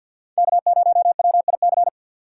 Morse code, heard as a keyed tone, about 54 wpm, 700 Hz. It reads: K0WIC
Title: USA Call Signs - 5 Characters - 4x Character Spacing - 50wpm